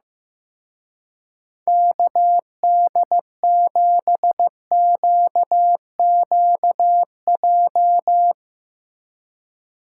Morse code, heard as a keyed tone, about 16 wpm, 700 Hz. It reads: KD7QQJ
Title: Speed Racing - USA Call Signs 15wpm